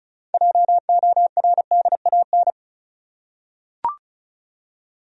Morse code, heard as a keyed tone, about 34 wpm, 700 Hz. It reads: JORDAN
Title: USA Common Names - 35wpm